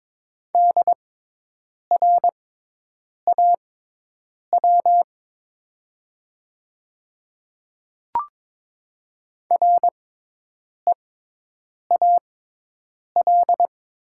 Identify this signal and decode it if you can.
Morse code, heard as a keyed tone, about 22 wpm, spaced as for 8 wpm, 700 Hz. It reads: DRAW REAL
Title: 6x Character Spacing - Top 300 Words - 22wpm